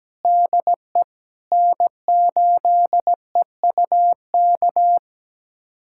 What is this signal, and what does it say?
Morse code, heard as a keyed tone, about 17 wpm, 700 Hz. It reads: DE N8EUK
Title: USA Callsign DE Callsign - 5 Characters - 17wpm